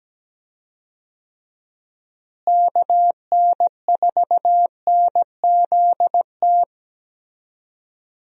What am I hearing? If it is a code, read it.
Morse code, heard as a keyed tone, about 17 wpm, 700 Hz. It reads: KN4NZT
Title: USA Call Signs - 6 Characters - Repeated 3x - 17wpm